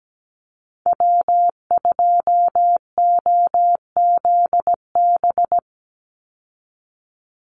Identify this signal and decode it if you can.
Morse code, heard as a keyed tone, about 17 wpm, 700 Hz. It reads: W2OZB
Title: Call Signs Repeated 3x - 17wpm